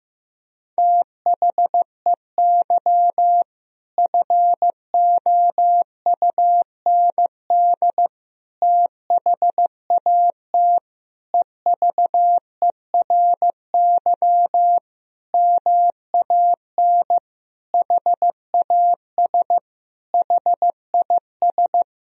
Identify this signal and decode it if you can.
Morse code, heard as a keyed tone, about 15 wpm, 700 Hz. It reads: THEY FOUND THAT EVERY MAN HAS HIS
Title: Sentences from Top 1000 Words - Review Part 4 - 15wpm